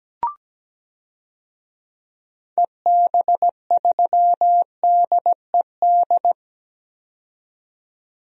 Morse code, heard as a keyed tone, about 17 wpm, 700 Hz. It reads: EB3DED
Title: DXCC-100 Call Signs - 6 Characters - 17wpm